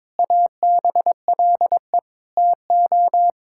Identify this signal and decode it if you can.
Morse code, heard as a keyed tone, about 22 wpm, 700 Hz. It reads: ABLE TO
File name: practice / 22wpm / Sets of 6 Words - 22wpm